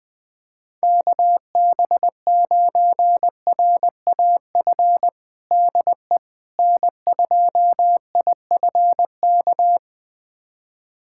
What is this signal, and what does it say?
Morse code, heard as a keyed tone, about 20 wpm, 700 Hz. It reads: KB9RAF DE N2IFK
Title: USA Callsign DE Callsign 20wpm